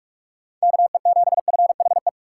Morse code, heard as a keyed tone, about 46 wpm, 700 Hz. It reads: KE7FHE
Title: USA Call Signs - 2x Character Spacing - 45wpm